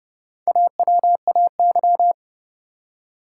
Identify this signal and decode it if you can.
Morse code, heard as a keyed tone, about 30 wpm, 700 Hz. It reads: AWAY
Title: Four Letter Words - 30wpm